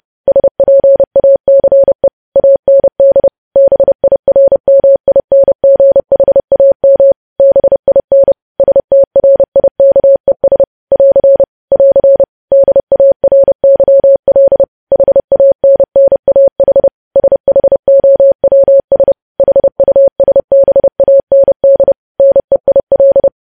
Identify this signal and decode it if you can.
Morse code, heard as a keyed tone, about 30 wpm, 550 Hz. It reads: SPACE AND BIRMINGHAM BIN STRIKES <AR> <AR> DARYL HANNAH SHOWS HUSBAND NEIL
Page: news headlines in morse code at 30 wpm